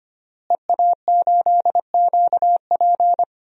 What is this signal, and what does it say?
Morse code, heard as a keyed tone, about 25 wpm, 700 Hz. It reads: EA8QP
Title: DXCC-100 Call Signs - 5 Characters - Repeated 3x - 25wpm